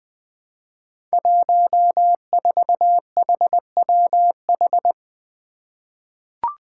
Morse code, heard as a keyed tone, about 20 wpm, 700 Hz. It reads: 14HWH
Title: Five Challenging Letter-Number Combinations 20wpm